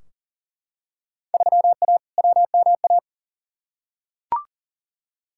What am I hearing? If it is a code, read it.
Morse code, heard as a keyed tone, about 40 wpm, 700 Hz. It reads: 3A WMA
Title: Field Day Exchanges Encore - 40wpm